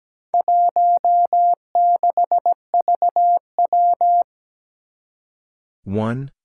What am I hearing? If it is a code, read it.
Morse code, heard as a keyed tone, about 17 wpm, 700 Hz. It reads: 16VW
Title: Four Challenging Letter-Number Combinations 17wpm